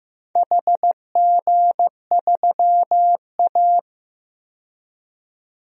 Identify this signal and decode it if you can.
Morse code, heard as a keyed tone, about 15 wpm, 700 Hz. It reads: HG3A
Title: DXCC-100 Call Signs - 4 Characters - 15wpm